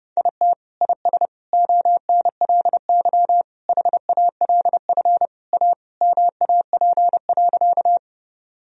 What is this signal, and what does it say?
Morse code, heard as a keyed tone, about 30 wpm, 700 Hz. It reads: IT IS ONLY HALF A MAP.
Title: Sentences from Top 1000 Words - Review Part 2 - 30wpm